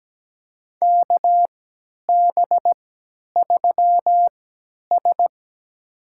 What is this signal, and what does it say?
Morse code, heard as a keyed tone, about 17 wpm, spaced as for 10 wpm, 700 Hz. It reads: KB3S
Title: USA Call Signs - 4 Characters - 3x Character Spacing - 17wpm